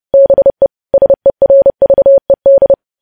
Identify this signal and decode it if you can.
Morse code, heard as a keyed tone, about 30 wpm, 550 Hz. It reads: BE SERVED
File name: podcast / News headlines-1x-30x30